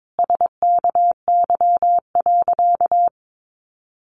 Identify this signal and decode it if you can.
Morse code, heard as a keyed tone, about 22 wpm, 700 Hz. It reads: SKY.
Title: Sentences from Top 1200 Words 22wpm